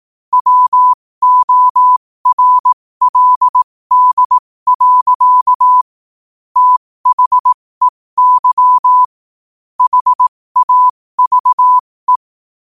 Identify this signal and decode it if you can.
Morse code, heard as a keyed tone, about 18 wpm, spaced as for 15 wpm, 1000 Hz. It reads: WORLD. THEY HAVE
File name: qotdfiles / 15wpm